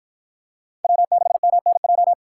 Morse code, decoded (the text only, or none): W6MNJ